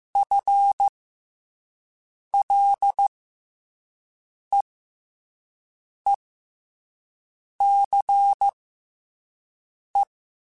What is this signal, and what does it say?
Morse code, heard as a keyed tone, about 15 wpm, spaced as for 5 wpm, 800 Hz. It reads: FLEECE
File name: FL / FL mp3s / FL Prefixes 1x1 @5